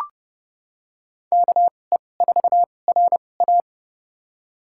Morse code, heard as a keyed tone, about 30 wpm, spaced as for 22 wpm, 700 Hz. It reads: KE4RA
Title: USA Call Signs - 2x Character Spacing - 30wpm